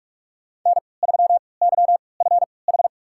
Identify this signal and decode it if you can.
Morse code, heard as a keyed tone, about 42 wpm, spaced as for 25 wpm, 700 Hz. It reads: N3YFH
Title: USA Call Signs - 5 Characters - 3x Character Spacing - 45wpm